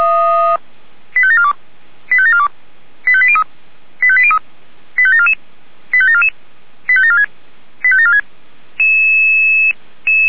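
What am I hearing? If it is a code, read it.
Five-tone selective call (ZVEI), heard as quick runs of five tones, tones of about 70 ms. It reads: 85742 85742 85803 85803 85744 85744 85746 85746
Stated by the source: ZVEI five-tone selective call received on an SDR.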